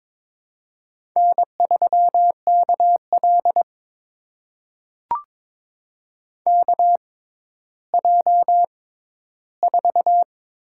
Morse code, heard as a keyed tone, about 22 wpm, spaced as for 13 wpm, 700 Hz. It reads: N3KL KJ4
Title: USA Call Signs - 6x Character Spacing - 22wpm